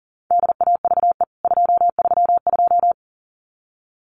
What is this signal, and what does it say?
Morse code, heard as a keyed tone, about 40 wpm, 700 Hz. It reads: DAVE 232
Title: CWT Exchange 40wpm